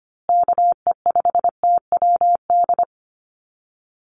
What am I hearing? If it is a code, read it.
Morse code, heard as a keyed tone, about 25 wpm, 700 Hz. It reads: KE5TWD
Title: USA Call Signs Repeated 3x 25wpm